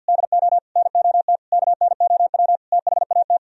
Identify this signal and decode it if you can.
Morse code, heard as a keyed tone, about 62 wpm, 700 Hz. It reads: DO NOT KNOW THAT
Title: Speed-Racing - Sets of 5 Words 50wpm